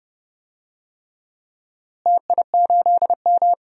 Morse code, heard as a keyed tone, about 30 wpm, 700 Hz. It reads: TI8M
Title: DXCC-100 Call Signs - 4 Characters - Repeated 3x - 30wpm